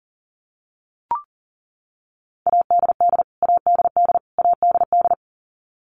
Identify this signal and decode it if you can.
Morse code, heard as a keyed tone, about 40 wpm, 700 Hz. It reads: ADD ADD ADD
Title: Top 500 Words Repeated 40wpm